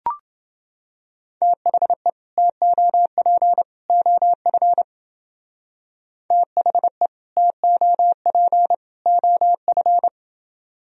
Morse code, heard as a keyed tone, about 29 wpm, 700 Hz. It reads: THE TOP OF THE TOP OF